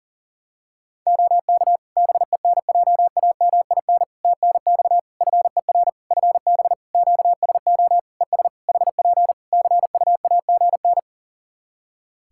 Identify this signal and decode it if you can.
Morse code, heard as a keyed tone, about 40 wpm, 700 Hz. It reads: OK BENJAMIN TNX FER FB QSO ES HP CUAGN
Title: K1USN SST QSO Practice 40wpm